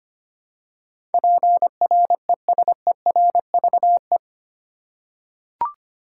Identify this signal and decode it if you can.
Morse code, heard as a keyed tone, about 25 wpm, 700 Hz. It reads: PRESERVE
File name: 8 Letter Words - First 3 Letters Given - 25wpm